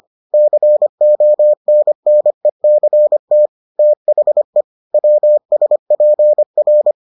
Morse code, heard as a keyed tone, about 25 wpm, 600 Hz. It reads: CONNECT THE WSPR